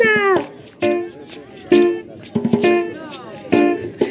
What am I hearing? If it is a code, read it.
Speech. There is no code.